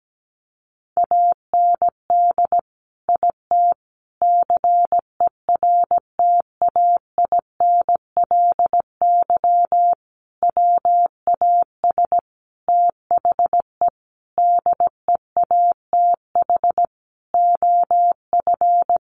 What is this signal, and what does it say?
Morse code, heard as a keyed tone, about 17 wpm, 700 Hz. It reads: AND IT CERTAINLY WAS THE DEATH OF
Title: Sentences from Top 1100 Words 17wpm